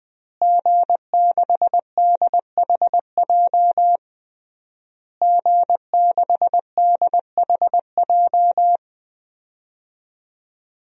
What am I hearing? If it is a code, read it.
Morse code, heard as a keyed tone, about 20 wpm, 700 Hz. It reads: G6DHJ G6DHJ
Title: DXCC-100 Call Signs - 5 Characters - Repeated 3x - 20wpm